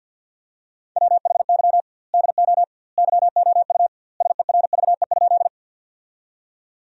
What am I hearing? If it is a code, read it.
Morse code, heard as a keyed tone, about 50 wpm, 700 Hz. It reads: WHY DO YOU SERVE?
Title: 2x Word-Spacing - Sentences from Top 400 Words - 50wpm